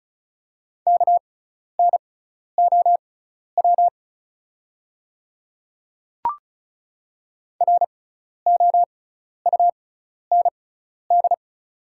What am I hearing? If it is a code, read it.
Morse code, heard as a keyed tone, about 34 wpm, spaced as for 12 wpm, 700 Hz. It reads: KNOW ROUND
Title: 6x Character Spacing - Top 200 Words - 35wpm